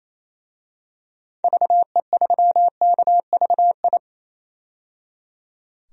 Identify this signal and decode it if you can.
Morse code, heard as a keyed tone, about 28 wpm, 700 Hz. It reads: VE3KVI
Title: DXCC-100 Call Signs - 6 Characters - 28wpm